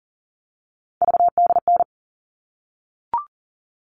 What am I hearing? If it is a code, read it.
Morse code, heard as a keyed tone, about 40 wpm, 700 Hz. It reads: VDN
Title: All Cut Numbers - 3-Digit 40wpm